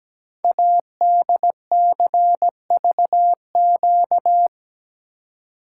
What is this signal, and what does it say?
Morse code, heard as a keyed tone, about 17 wpm, 700 Hz. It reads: ADCVQ
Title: Code Groups - 5 Letters - 17wpm